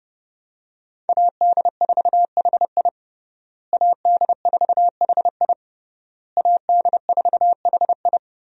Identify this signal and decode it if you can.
Morse code, heard as a keyed tone, about 30 wpm, 700 Hz. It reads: AD4HI AD4HI AD4HI
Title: USA Call Signs - 5 Characters - Repeated 3x - 30wpm